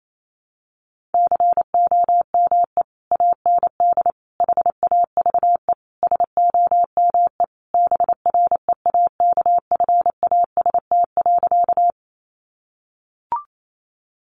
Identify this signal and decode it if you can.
Morse code, heard as a keyed tone, about 28 wpm, 700 Hz. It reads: COME AND HAVE SOME BREAKFAST.
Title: Sentences from Top 1500 Words 28wpm